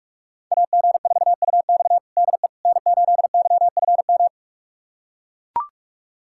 Morse code, heard as a keyed tone, about 41 wpm, 700 Hz. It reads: AG4UX DE N8YFM
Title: USA Callsign DE Callsign - 5 Characters - 45wpm